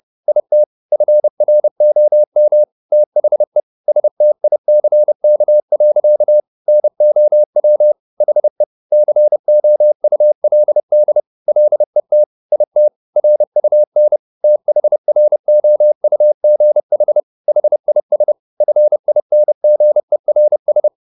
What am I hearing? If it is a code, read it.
Morse code, heard as a keyed tone, about 30 wpm, 600 Hz. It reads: IT FROM THE STICK. NOW HE COULD LET IT RUN THROUGH HIS FINGERS
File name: the old man and the sea / the old man and the sea 1 30wpm